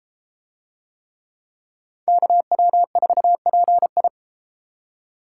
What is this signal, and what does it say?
Morse code, heard as a keyed tone, about 33 wpm, 700 Hz. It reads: KW4PI